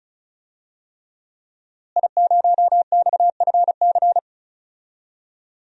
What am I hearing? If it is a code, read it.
Morse code, heard as a keyed tone, about 35 wpm, 700 Hz. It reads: I0XFC
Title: DXCC-100 Call Signs - 5 Characters - Repeated 3x - 35wpm